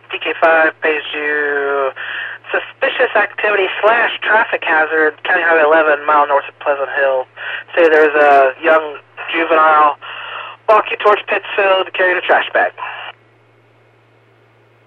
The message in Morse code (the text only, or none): none